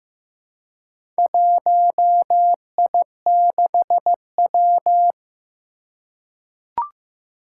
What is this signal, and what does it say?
Morse code, heard as a keyed tone, about 15 wpm, 700 Hz. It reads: 1I6W
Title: Four Challenging Letter-Number Combinations 15wpm